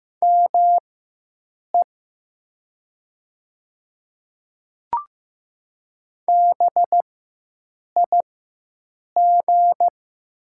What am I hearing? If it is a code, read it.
Morse code, heard as a keyed tone, about 15 wpm, spaced as for 7 wpm, 700 Hz. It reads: ME BIG
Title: 4x Character Spacing - Top 300 Words - 15wpm